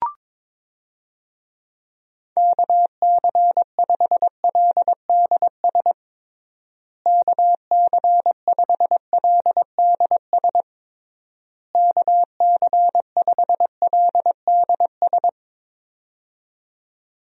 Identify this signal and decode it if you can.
Morse code, heard as a keyed tone, about 21 wpm, 700 Hz. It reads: KC5LDS KC5LDS KC5LDS
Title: USA Call Signs - 6 Characters - Repeated 3x - 22wpm